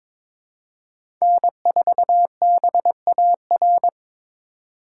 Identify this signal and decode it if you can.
Morse code, heard as a keyed tone, about 22 wpm, 700 Hz. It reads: N4BAR
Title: USA Call Signs - 5 Characters - 4x Character Spacing - 22wpm